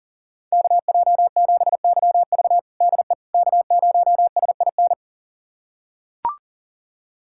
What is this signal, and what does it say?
Morse code, heard as a keyed tone, about 40 wpm, 700 Hz. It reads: KJ7YV DE K0SIN
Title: USA Callsign DE Callsign - 5 Characters - 40wpm